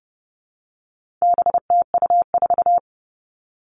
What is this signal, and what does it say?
Morse code, heard as a keyed tone, about 30 wpm, 700 Hz. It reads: BTU4
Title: All Cut Numbers - 4-Digit 30wpm